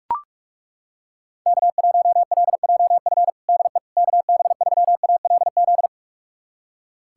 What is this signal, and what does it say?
Morse code, heard as a keyed tone, about 41 wpm, 700 Hz. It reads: K1LJF DE KB3ALZ
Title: USA Callsign DE Callsign 45wpm